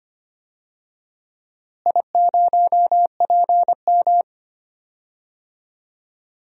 Morse code, heard as a keyed tone, about 25 wpm, 700 Hz. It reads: I0PM
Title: DXCC-100 Call Signs - 4 Characters - Repeated 2x - 25wpm